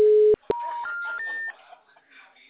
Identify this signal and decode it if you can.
Special information tone.